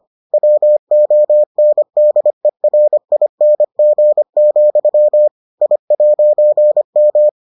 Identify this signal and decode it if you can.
Morse code, heard as a keyed tone, about 25 wpm, 600 Hz. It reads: WONDERING, I'M